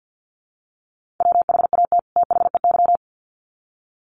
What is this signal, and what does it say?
Morse code, heard as a keyed tone, about 48 wpm, 700 Hz. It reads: WHAT THEY
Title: Sets of 2 Words 50wpm